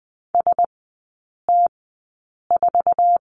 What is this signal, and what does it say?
Morse code, heard as a keyed tone, about 20 wpm, spaced as for 8 wpm, 700 Hz. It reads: ST4
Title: 047 Lesson 9e - Number 4 - Three Characters 20wpm